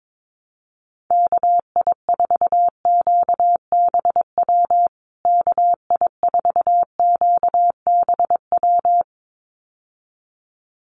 Morse code, heard as a keyed tone, about 22 wpm, 700 Hz. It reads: KI4QBW KI4QBW
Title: USA Call Signs Repeated 2x 22wpm